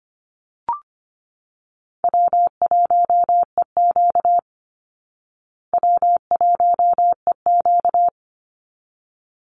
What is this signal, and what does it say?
Morse code, heard as a keyed tone, about 25 wpm, 700 Hz. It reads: W1EQ W1EQ
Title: Easy Call Signs Repeated 2x - 25wpm